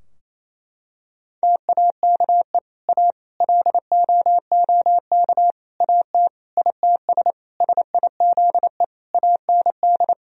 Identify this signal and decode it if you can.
Morse code, heard as a keyed tone, about 28 wpm, 700 Hz. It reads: TAKE A LOOK AT ITS SIZE AND
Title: Sentences from Top 1000 Words - Review Part 2 - 28wpm